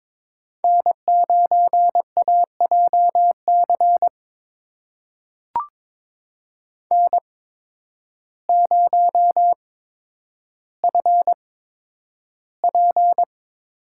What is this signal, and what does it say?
Morse code, heard as a keyed tone, about 22 wpm, spaced as for 10 wpm, 700 Hz. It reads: N9AJC N0FP
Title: USA Call Signs - 5 Characters - 8x Character Spacing - 22wpm